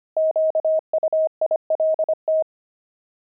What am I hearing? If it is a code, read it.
Morse code, heard as a keyed tone, about 25 wpm, 630 Hz. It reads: QUILT
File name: pangram-fs-25-cs-25-fragment